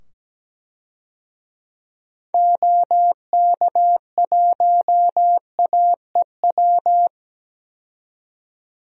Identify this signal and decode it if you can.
Morse code, heard as a keyed tone, about 17 wpm, 700 Hz. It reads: OK1AEW
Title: DXCC-100 Call Signs - 6 Characters - 17wpm